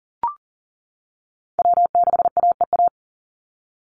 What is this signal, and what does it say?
Morse code, heard as a keyed tone, about 40 wpm, 700 Hz. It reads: W6AEA
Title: Call Signs - 40wpm